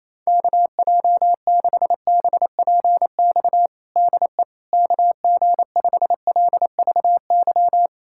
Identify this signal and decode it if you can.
Morse code, heard as a keyed tone, about 27 wpm, 700 Hz. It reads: KJ6BPX DE KG5LVY
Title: USA Callsign DE Callsign - 6 Characters - 28wpm